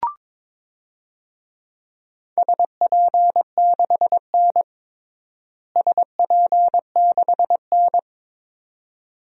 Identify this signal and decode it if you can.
Morse code, heard as a keyed tone, about 22 wpm, 700 Hz. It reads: SP6N SP6N
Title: DXCC-100 Call Signs - 4 Characters - Repeated 2x - 22wpm